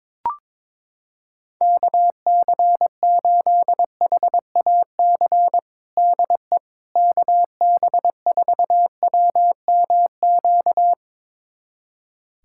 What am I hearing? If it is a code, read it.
Morse code, heard as a keyed tone, about 22 wpm, 700 Hz. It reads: KC8HAC DE KB4WMQ
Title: USA Callsign DE Callsign - 6 Characters - 22wpm